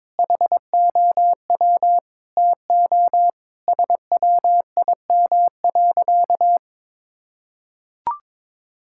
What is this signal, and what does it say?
Morse code, heard as a keyed tone, about 22 wpm, 700 Hz. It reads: HOW TO SWIM.
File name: practice / 22wpm / Sentences with 4-Word Phrases - 22wpm